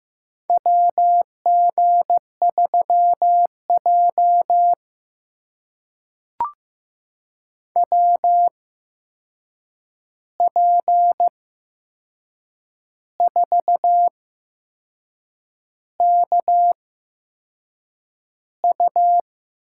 Morse code, heard as a keyed tone, about 15 wpm, spaced as for 6 wpm, 700 Hz. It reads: WG3J WP4KU
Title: USA Call Signs - 8x Character Spacing - 15wpm